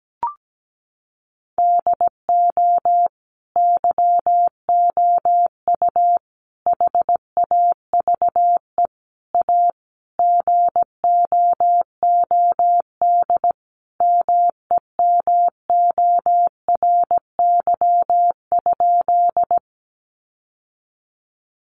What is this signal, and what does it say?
Morse code, heard as a keyed tone, about 17 wpm, 700 Hz. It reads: DO YOU HAVE A GOOD MEMORY?
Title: Sentences from Top 1300 Words 17wpm